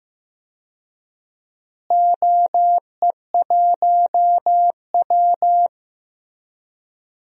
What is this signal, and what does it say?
Morse code, heard as a keyed tone, about 15 wpm, 700 Hz. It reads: OE1W
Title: DXCC-100 Call Signs - 4 Characters - Repeated 2x - 15wpm